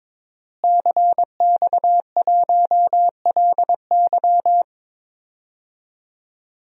Morse code, heard as a keyed tone, about 22 wpm, 700 Hz. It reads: CX1LY
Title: DXCC-100 Call Signs - 5 Characters - Repeated 2x - 22wpm